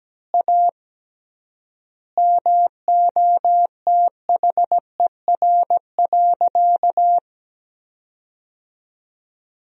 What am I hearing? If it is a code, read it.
Morse code, heard as a keyed tone, about 17 wpm, 700 Hz. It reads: A MOTHER.
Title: 3x Word-Spacing - Sentences from Top 200 Words - 17wpm